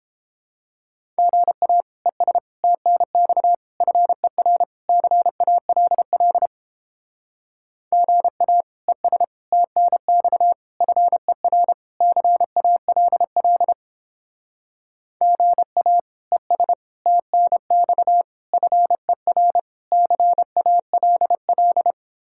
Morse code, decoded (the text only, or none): GA ES TNX FER CALL GA ES TNX FER CALL GA ES TNX FER CALL